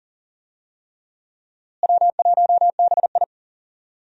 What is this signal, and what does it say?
Morse code, heard as a keyed tone, about 40 wpm, 700 Hz. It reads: W1BI
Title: USA Call Signs - 4 Characters - Repeated 3x - 40wpm